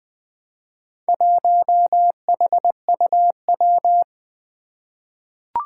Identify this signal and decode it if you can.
Morse code, heard as a keyed tone, about 20 wpm, 700 Hz. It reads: 1HUW
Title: Four Challenging Letter-Number Combinations 20wpm